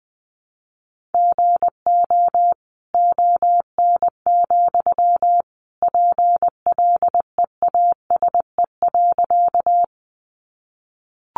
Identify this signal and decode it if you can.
Morse code, heard as a keyed tone, about 20 wpm, 700 Hz. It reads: GO ON, PLEASE.
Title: Sentences from Top 1500 Words 20wpm